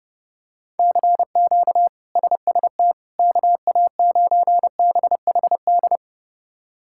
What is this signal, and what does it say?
Morse code, heard as a keyed tone, about 30 wpm, 700 Hz. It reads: CQ SST KA9BHD
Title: K1USN SST QSO Practice - 30wpm